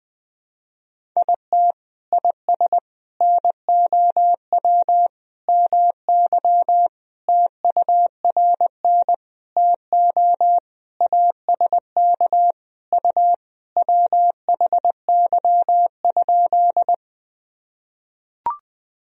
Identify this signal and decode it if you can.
Morse code, heard as a keyed tone, about 20 wpm, 700 Hz. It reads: IT IS NOW MY TURN TO ASK U WHY?
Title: Sentences from Top 200 Words and Abbreviations 20wpm